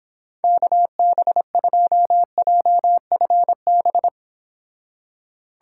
Morse code, heard as a keyed tone, about 26 wpm, 700 Hz. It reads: KB2JFB